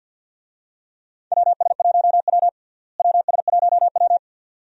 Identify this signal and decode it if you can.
Morse code, heard as a keyed tone, about 52 wpm, 700 Hz. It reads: WS1W WS1W
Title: USA Call Signs - 4 Characters - Repeated 2x - 50wpm